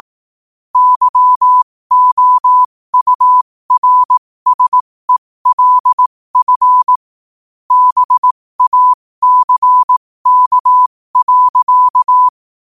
Morse code, heard as a keyed tone, about 18 wpm, spaced as for 15 wpm, 1000 Hz. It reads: YOURSELF BACK.
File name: qotdfiles / 15wpm